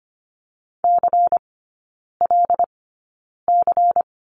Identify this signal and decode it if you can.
Morse code, heard as a keyed tone, about 25 wpm, spaced as for 9 wpm, 700 Hz. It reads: CLC
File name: course / 25:10wpm / 093 Lesson 16e - Letter c - Three Characters 25wpm